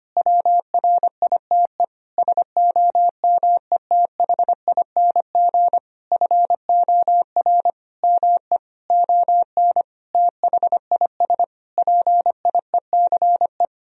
Morse code, heard as a keyed tone, about 25 wpm, 700 Hz. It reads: WRITE SOMETHING FOR ME ON THIS PIECE
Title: Sentences from Top 1000 Words - Review Part 2 - 25wpm